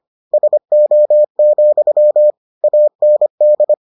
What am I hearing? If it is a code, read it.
Morse code, heard as a keyed tone, about 25 wpm, 600 Hz. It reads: SO, AND